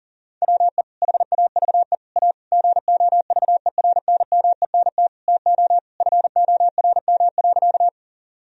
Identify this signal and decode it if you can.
Morse code, heard as a keyed tone, about 40 wpm, 700 Hz. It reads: WE HAVE A GOVERNMENT TO FORM.
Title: Sentences from Top 1000 Words - Review Part 1 - 40wpm